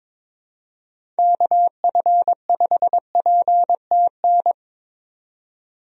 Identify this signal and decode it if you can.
Morse code, heard as a keyed tone, about 21 wpm, 700 Hz. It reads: KF5PTN
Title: USA Call Signs - 6 Characters - 22wpm